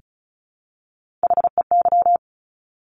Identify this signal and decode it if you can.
Morse code, heard as a keyed tone, about 36 wpm, 700 Hz. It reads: HEY
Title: Top Three Letter Combinations in Words 35wpm